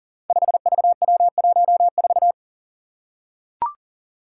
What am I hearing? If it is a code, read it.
Morse code, heard as a keyed tone, about 42 wpm, 700 Hz. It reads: 5VW14
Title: Five Challenging Letter-Number Combinations 40wpm